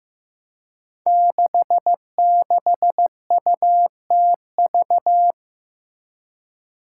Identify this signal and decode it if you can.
Morse code, heard as a keyed tone, about 15 wpm, 700 Hz. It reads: 66UTV
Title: Five Challenging Letter-Number Combinations 15wpm